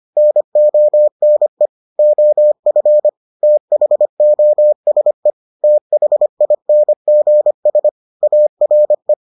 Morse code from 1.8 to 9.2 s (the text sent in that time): OF THOSE THINGS ARE